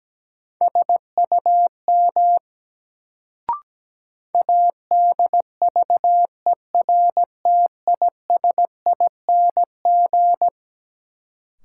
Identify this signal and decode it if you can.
Morse code, heard as a keyed tone, about 17 wpm, 700 Hz. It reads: SUM ADVERTISING
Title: Sentences from Top 1000 Words - Review Part 6 - New Word Drill - 17wpm